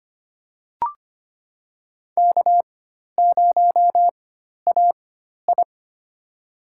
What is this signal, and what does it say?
Morse code, heard as a keyed tone, about 25 wpm, spaced as for 12 wpm, 700 Hz. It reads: K0AI
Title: USA Call Signs - 4x Character Spacing - 25wpm